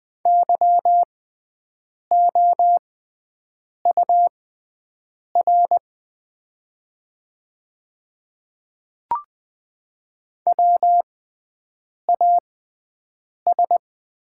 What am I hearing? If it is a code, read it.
Morse code, heard as a keyed tone, about 20 wpm, spaced as for 7 wpm, 700 Hz. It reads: YOUR WAS
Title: 6x Character Spacing - Top 100 Words - 20wpm